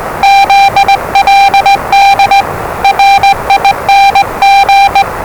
Morse code, heard as a keyed tone, about 20 wpm, 790 Hz, with noise. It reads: ZLD RING
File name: WellingtonRadioZlwCloseDownOnHfAndMf / 01-Zlw 4277khz 417.5 qrt until 0925z